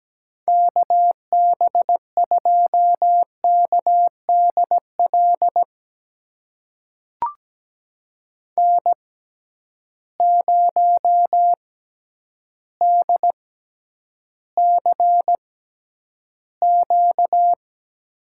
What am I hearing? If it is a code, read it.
Morse code, heard as a keyed tone, about 17 wpm, spaced as for 9 wpm, 700 Hz. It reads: KB2KDL N0DCQ
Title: USA Call Signs - 6x Character Spacing - 17wpm